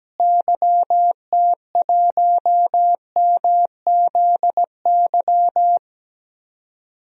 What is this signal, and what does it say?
Morse code, heard as a keyed tone, about 17 wpm, 700 Hz. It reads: YT1MZY
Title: DXCC-100 Call Signs - 6 Characters - Repeated 2x - 17wpm